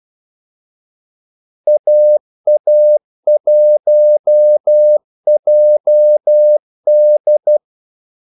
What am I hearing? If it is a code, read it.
Morse code, heard as a keyed tone, about 12 wpm, 600 Hz. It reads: AA1JD
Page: 2 ekstra ordmellemrum mellem kaldesignalerne